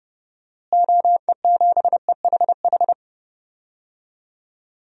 Morse code, heard as a keyed tone, about 32 wpm, 700 Hz. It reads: OE7EHH
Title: DXCC-100 Call Signs - 6 Characters - Repeated 3x - 30wpm